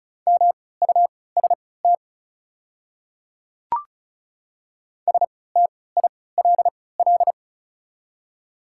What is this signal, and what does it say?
Morse code, heard as a keyed tone, about 34 wpm, spaced as for 20 wpm, 700 Hz. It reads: MUST STILL
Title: 3x Character Spacing - Top 300 Words - 35wpm